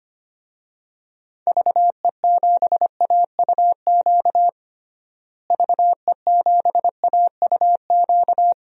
Morse code, heard as a keyed tone, about 25 wpm, 700 Hz. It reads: VE7AUQ VE7AUQ
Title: DXCC-100 Call Signs - 6 Characters - Repeated 3x - 25wpm